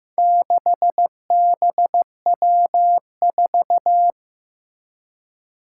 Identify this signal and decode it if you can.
Morse code, heard as a keyed tone, about 15 wpm, 700 Hz. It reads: 6BW4
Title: Four Challenging Letter-Number Combinations 15wpm